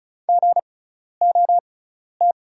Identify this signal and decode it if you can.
Morse code, heard as a keyed tone, about 36 wpm, spaced as for 12 wpm, 690 Hz. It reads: GOT
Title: 6x Character Spacing - Top 500 Words - 35wpm